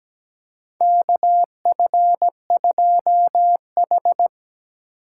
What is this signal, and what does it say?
Morse code, heard as a keyed tone, about 17 wpm, 700 Hz. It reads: KF2H
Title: USA Call Signs - 4 Characters - 2x Character Spacing - 17wpm